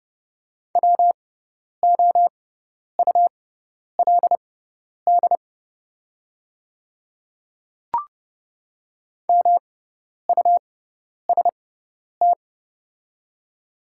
Morse code, heard as a keyed tone, about 30 wpm, spaced as for 10 wpm, 700 Hz. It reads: WOULD MUST
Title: 6x Character Spacing - Top 300 Words - 30wpm